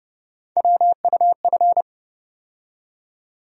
Morse code, heard as a keyed tone, about 30 wpm, 700 Hz. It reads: WUF
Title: USA Call Sign Postfixes - 3 Letters 30wpm